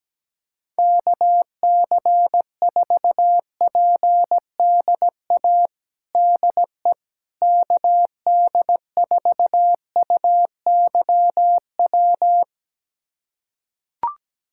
Morse code, heard as a keyed tone, about 17 wpm, 700 Hz. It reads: KC4PDA DE KD4UYW
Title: USA Callsign DE Callsign - 6 Characters - 17wpm